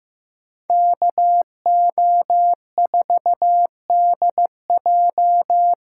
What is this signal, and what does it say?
Morse code, heard as a keyed tone, about 18 wpm, 700 Hz. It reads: KO4DJ
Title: Speed Racing - USA 2x2 Call Signs 15wpm